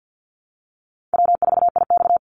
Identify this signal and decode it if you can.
Morse code, heard as a keyed tone, about 50 wpm, 700 Hz. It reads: W4IX
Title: Easy Call Signs Repeated 3x - 50wpm